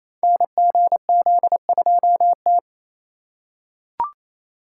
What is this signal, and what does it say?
Morse code, heard as a keyed tone, about 29 wpm, 700 Hz. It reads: NGZ2T
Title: Code Groups - 5 Letter:Numbers - 28wpm